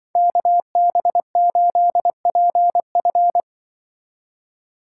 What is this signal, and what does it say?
Morse code, heard as a keyed tone, about 23 wpm, 700 Hz. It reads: KB8PF
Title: Speed Racing - USA 2x2 Call Signs 20wpm